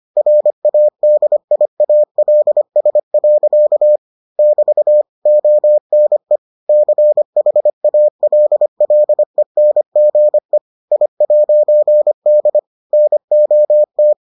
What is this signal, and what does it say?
Morse code, heard as a keyed tone, about 25 wpm, 600 Hz. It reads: RADIALS. <BT> ONE CHALLENGE I'D NOT